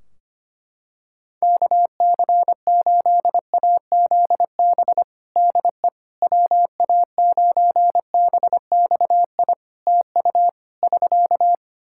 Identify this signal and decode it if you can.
Morse code, heard as a keyed tone, about 25 wpm, 700 Hz. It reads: KC8AZB DE WA9BXI TU <SK>
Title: K1USN SST QSO Practice 25wpm